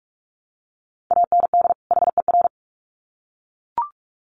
Morse code, heard as a keyed tone, about 44 wpm, 700 Hz. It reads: AND HER
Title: Sets of 2 Words 45wpm